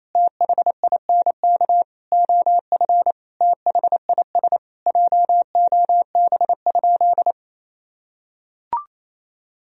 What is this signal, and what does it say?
Morse code, heard as a keyed tone, about 28 wpm, 700 Hz. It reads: THINK OF THIS JOB?
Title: Sentences from Top 1000 Words - Review Part 3 - 28wpm